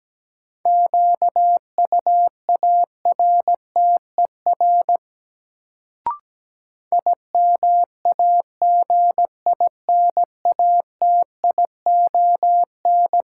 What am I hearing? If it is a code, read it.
Morse code, heard as a keyed tone, about 17 wpm, 700 Hz. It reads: QUARTER IMAGINATION
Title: Sentences from Top 1300 Words - New Word Drill Alt - 17wpm